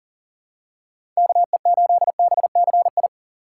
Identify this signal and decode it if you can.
Morse code, heard as a keyed tone, about 40 wpm, 700 Hz. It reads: KE8BCI
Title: USA Call Signs - 6 Characters - 6x Character Spacing - 40wpm